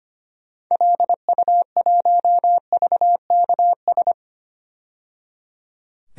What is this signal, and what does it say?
Morse code, heard as a keyed tone, about 25 wpm, 700 Hz. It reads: LU1VKS
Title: DXCC-100 Call Signs - 6 Characters - 25wpm